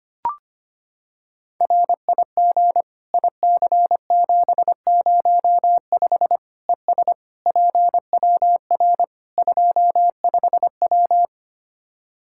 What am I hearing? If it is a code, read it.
Morse code, heard as a keyed tone, about 25 wpm, 700 Hz. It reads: RIG IC705 ES PWR 25W
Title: Practice QSO - Exchange 2 - Info - 25wpm